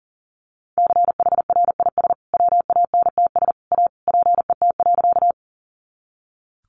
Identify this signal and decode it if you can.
Morse code, heard as a keyed tone, about 40 wpm, 700 Hz. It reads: CHRIS WANTS A PET.
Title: Easy Phrases 40wpm